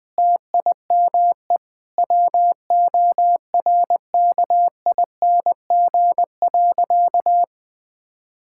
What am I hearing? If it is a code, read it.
Morse code, heard as a keyed tone, about 20 wpm, 700 Hz. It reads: TIME WORKING.
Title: Sentences with 5-Word Phrases - 20wpm